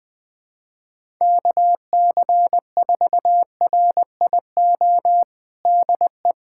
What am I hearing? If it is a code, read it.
Morse code, heard as a keyed tone, about 21 wpm, 700 Hz. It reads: KC4RIO DE
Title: USA Callsign DE Callsign 20wpm